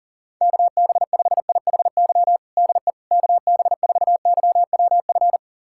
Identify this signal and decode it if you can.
Morse code, heard as a keyed tone, about 40 wpm, 700 Hz. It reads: KB5IHY DE KB4YWF
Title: USA Callsign DE Callsign - 6 Characters - 40wpm